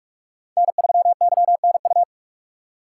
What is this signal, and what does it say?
Morse code, heard as a keyed tone, about 44 wpm, 700 Hz. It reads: N3YNU